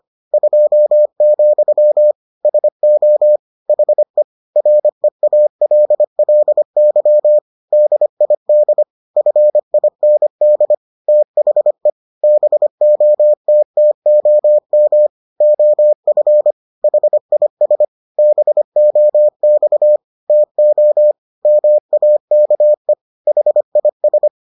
Morse code, heard as a keyed tone, about 25 wpm, 600 Hz. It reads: 2, SO HE REALLY DID FIND THE BOTTOM OF HIS BOX TO MAKE HIS